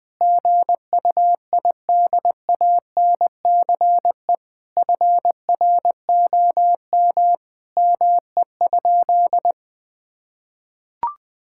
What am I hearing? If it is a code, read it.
Morse code, heard as a keyed tone, about 20 wpm, 700 Hz. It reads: GUIDANCE FROM ME?
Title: Sentences from Top 1600 Words 20wpm